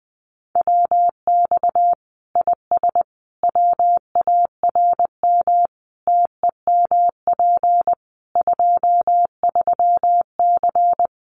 Morse code, heard as a keyed tone, about 20 wpm, 700 Hz. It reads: WX IS WARM TEMP 23C
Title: Common QSO Phrases 20wpm